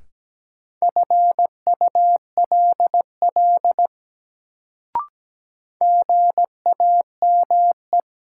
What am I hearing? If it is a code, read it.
Morse code, heard as a keyed tone, about 17 wpm, 700 Hz. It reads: FULL GAME
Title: Sentences from Top 500 Words - New Word Drill - 17wpm